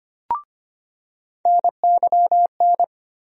Morse code, heard as a keyed tone, about 25 wpm, 700 Hz. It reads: NYN
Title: USA Call Sign Postfixes - 3 Letters 25wpm